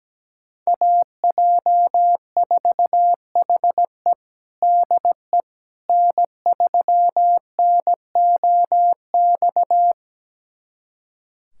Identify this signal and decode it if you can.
Morse code, heard as a keyed tone, about 17 wpm, 700 Hz. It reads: AJ4HE DE N3NOX
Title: USA Callsign DE Callsign - 5 Characters - 17wpm